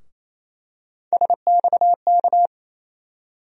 Morse code, heard as a keyed tone, about 28 wpm, 700 Hz. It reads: SXK
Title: USA Call Sign Postfixes - 3 Letters 28wpm